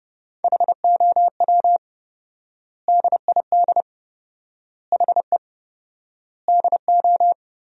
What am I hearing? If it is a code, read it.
Morse code, heard as a keyed tone, about 30 wpm, 700 Hz. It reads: HOW DID HE DO
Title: 4x Word-Spacing - Sentences from Top 100 Words - 30wpm